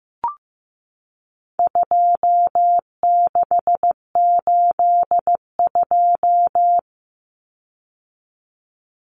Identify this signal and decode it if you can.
Morse code, heard as a keyed tone, about 15 wpm, 700 Hz. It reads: 2682
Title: Common Cut Numbers - 4-Digit 15wpm